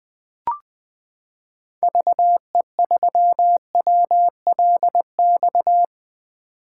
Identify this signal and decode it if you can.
Morse code, heard as a keyed tone, about 20 wpm, 700 Hz. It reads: VE3WLX
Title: 2023 POTA Activator Call Signs - 20wpm